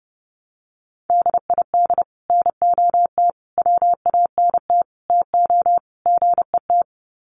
Morse code, heard as a keyed tone, about 30 wpm, 700 Hz. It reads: DID NOT WANT TO GET
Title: Sets of 5 Words Encore2 30wpm